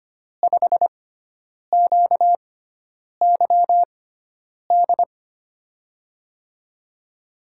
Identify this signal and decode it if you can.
Morse code, heard as a keyed tone, about 25 wpm, spaced as for 9 wpm, 700 Hz. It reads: 5QYD
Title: USA Call Signs - 6 Characters - 6x Character Spacing - 25wpm